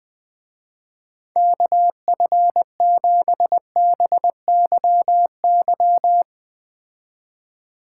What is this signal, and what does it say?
Morse code, heard as a keyed tone, about 21 wpm, 700 Hz. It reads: KF7BYY